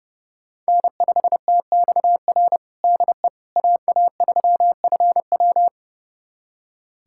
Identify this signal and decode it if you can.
Morse code, heard as a keyed tone, about 32 wpm, 700 Hz. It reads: N5TXR DE AA3FW
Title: USA Callsign DE Callsign - 5 Characters - 30wpm